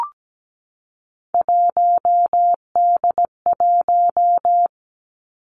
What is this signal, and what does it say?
Morse code, heard as a keyed tone, about 17 wpm, 700 Hz. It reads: 1D1
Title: Three Challenging Letter-Number Combinations 17wpm